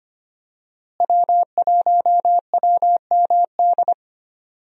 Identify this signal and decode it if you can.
Morse code, heard as a keyed tone, about 25 wpm, 700 Hz. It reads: W1WMD
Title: USA Call Signs - 5 Characters - Repeated 3x - 25wpm